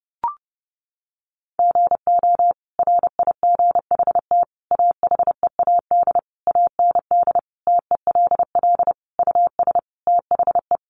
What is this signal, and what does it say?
Morse code, heard as a keyed tone, about 28 wpm, 700 Hz. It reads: GO RIGHT AHEAD AND TELL US THE
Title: Sentences from Top 1100 Words 30wpm